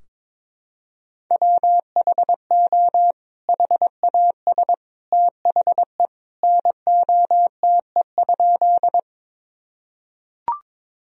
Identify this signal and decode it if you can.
Morse code, heard as a keyed tone, about 21 wpm, 700 Hz. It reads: WHO HAS THE NOTE?
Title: Sentences from Top 1000 Words - Review Part 2 - 22wpm